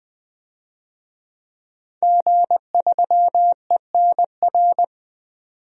Morse code, heard as a keyed tone, about 20 wpm, 700 Hz. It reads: G3ENR
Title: DXCC-100 Call Signs - 5 Characters - 20wpm